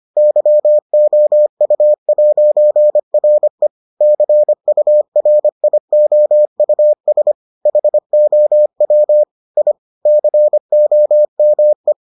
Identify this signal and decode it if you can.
Morse code, heard as a keyed tone, about 25 wpm, 600 Hz. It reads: YOU'RE CURIOUS HOW I COME